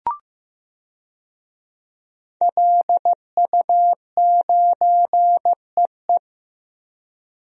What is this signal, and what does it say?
Morse code, heard as a keyed tone, about 15 wpm, 700 Hz. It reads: LU9EE
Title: DXCC-100 Call Signs - 5 Characters - Repeated 2x - 15wpm